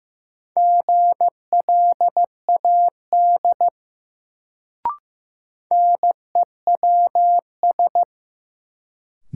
Morse code, heard as a keyed tone, about 15 wpm, 700 Hz. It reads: GLAD NEWS
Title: Sentences from Top 800 Words - New Word Drill - 15wpm